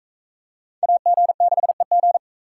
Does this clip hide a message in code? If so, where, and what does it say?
Yes: Morse code from 0.8 to 2.2 s, heard as a keyed tone, about 37 wpm, 700 Hz. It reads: AG6EG